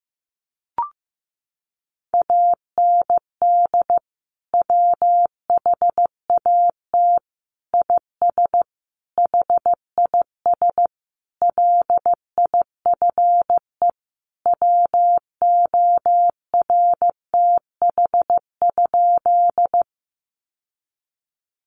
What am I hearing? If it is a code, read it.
Morse code, heard as a keyed tone, about 15 wpm, 700 Hz. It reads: AND WHAT IS HIS LIFE WORTH?
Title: Sentences from Top 800 Words 15wpm